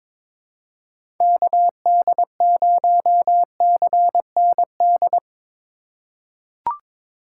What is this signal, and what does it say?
Morse code, heard as a keyed tone, about 23 wpm, 700 Hz. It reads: KD0CND
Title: USA Call Signs - 4x Character Spacing - 22wpm